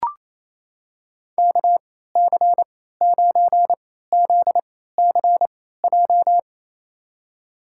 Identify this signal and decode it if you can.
Morse code, heard as a keyed tone, about 28 wpm, spaced as for 16 wpm, 700 Hz. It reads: KC9ZCJ
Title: USA Call Signs - 3x Character Spacing - 28wpm